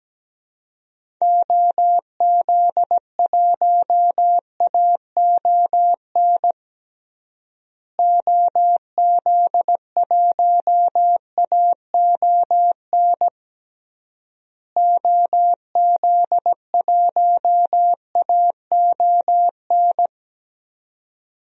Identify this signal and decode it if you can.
Morse code, heard as a keyed tone, about 17 wpm, 700 Hz. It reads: OZ1AON OZ1AON OZ1AON
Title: DXCC-100 Call Signs - 6 Characters - Repeated 3x - 17wpm